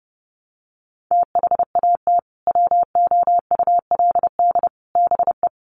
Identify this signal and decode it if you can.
Morse code, heard as a keyed tone, about 30 wpm, 700 Hz. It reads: THAT WOULD BE
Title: Sentences from Top 400 Words 30wpm